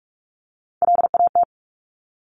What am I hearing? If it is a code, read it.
Morse code, heard as a keyed tone, about 46 wpm, 700 Hz. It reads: LAT
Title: Top Three Letter Combinations in Words 45wpm